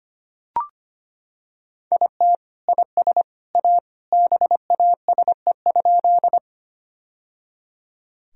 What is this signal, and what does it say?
Morse code, heard as a keyed tone, about 25 wpm, 700 Hz. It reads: IT IS A BASE?
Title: Sentences from Top 1000 Words - Review Part 1 - 25wpm